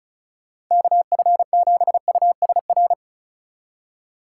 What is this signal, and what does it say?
Morse code, heard as a keyed tone, about 42 wpm, 700 Hz. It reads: KF7USR
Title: Speed Racing - USA 2x3 Call Signs 35wpm